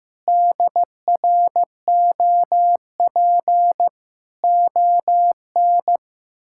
Morse code, heard as a keyed tone, about 15 wpm, 700 Hz. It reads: DROP ON
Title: Pangram Challenge v2 - 15wpm